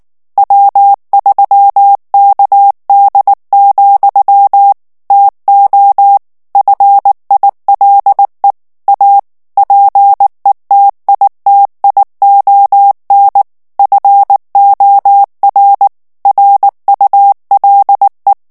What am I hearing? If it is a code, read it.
Morse code, heard as a keyed tone, about 19 wpm, 800 Hz. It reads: W3KD, TO FILE A PETITION FOR RULE
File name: ANGLAIS texte clair 3 vitesse 18 mots minute